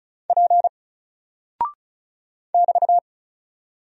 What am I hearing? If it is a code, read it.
Morse code, heard as a keyed tone, about 36 wpm, 700 Hz. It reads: P <BT>
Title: Single Character (Letter:Number:Punctuation:Prosign) Alt 35wpm